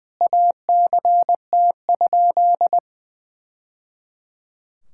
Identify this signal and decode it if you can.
Morse code, heard as a keyed tone, about 20 wpm, 700 Hz. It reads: ACT?
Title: Sentences from Top 1000 Words - Review Part 6 - 20wpm